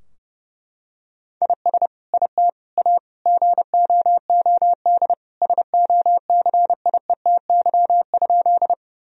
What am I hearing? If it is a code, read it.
Morse code, heard as a keyed tone, about 30 wpm, 700 Hz. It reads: IS IT A GOOD SOCIETY?
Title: Sentences from Top 1000 Words - Review Part 4 - 30wpm